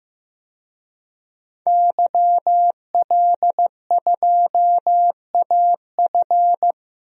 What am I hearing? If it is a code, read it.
Morse code, heard as a keyed tone, about 15 wpm, 700 Hz. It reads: YL2AF
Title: DXCC-100 Call Signs - 5 Characters - 15wpm